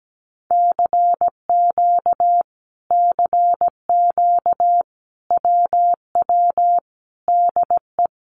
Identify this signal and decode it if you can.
Morse code, heard as a keyed tone, about 17 wpm, 700 Hz. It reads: CQ CQ WW DE
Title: Common QSO Phrases Encore 17wpm